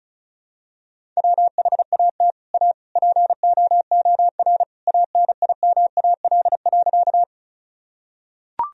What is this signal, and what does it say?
Morse code, heard as a keyed tone, about 36 wpm, 700 Hz. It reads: WHAT A POOR ANIMAL.
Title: Sentences from Top 1000 Words - Review Part 1 - 35wpm